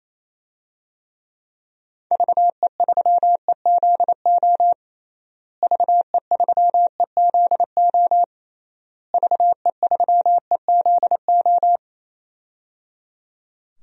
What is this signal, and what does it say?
Morse code, heard as a keyed tone, about 28 wpm, 700 Hz. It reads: VE3EZO VE3EZO VE3EZO
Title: DXCC-100 Call Signs - 6 Characters - Repeated 3x - 28wpm